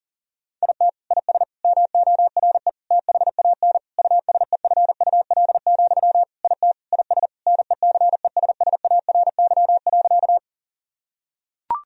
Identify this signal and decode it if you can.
Morse code, heard as a keyed tone, about 37 wpm, 700 Hz. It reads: IT IS MORE THAN USEFUL, IT IS NECESSARY.
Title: Sentences from Top 1000 Words - Review Part 4 - 40wpm